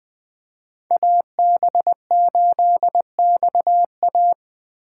Morse code, heard as a keyed tone, about 20 wpm, 700 Hz. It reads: AB8XA
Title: USA Call Signs - 5 Characters - 2x Character Spacing - 20wpm